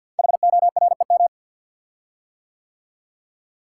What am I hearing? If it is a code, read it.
Morse code, heard as a keyed tone, about 52 wpm, 700 Hz. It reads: HOREM
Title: Code Groups - 5 Letters - 50wpm